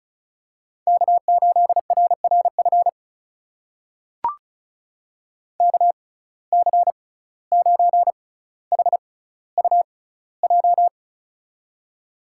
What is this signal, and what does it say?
Morse code, heard as a keyed tone, about 36 wpm, spaced as for 17 wpm, 700 Hz. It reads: K8RRF KC9HUJ
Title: USA Call Signs - 6x Character Spacing - 35wpm